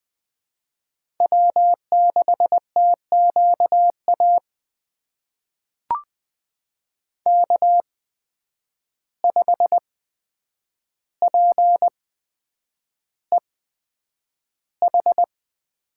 Morse code, heard as a keyed tone, about 20 wpm, spaced as for 9 wpm, 700 Hz. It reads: W6TQA K5PEH
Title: USA Call Signs - 5 Characters - 8x Character Spacing - 20wpm